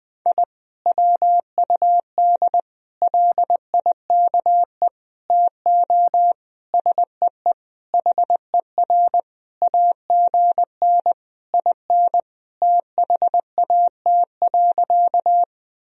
Morse code, heard as a keyed tone, about 20 wpm, 700 Hz. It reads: I WUD LIKE TO SEE HER AGN IN THAT.
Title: Sentences from Top 200 Words and Abbreviations 20wpm